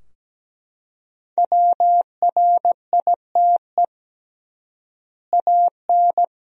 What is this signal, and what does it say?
Morse code, heard as a keyed tone, about 17 wpm, 700 Hz. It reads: WRITE AN
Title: 3x Word-Spacing - Sentences from Top 300 Words - 17wpm